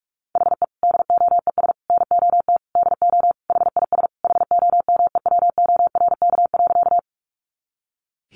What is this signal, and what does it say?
Morse code, heard as a keyed tone, about 44 wpm, 700 Hz. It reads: HE DOES NOT DO HIS HOMEWORK.
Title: Additional Easy Phrases 45wpm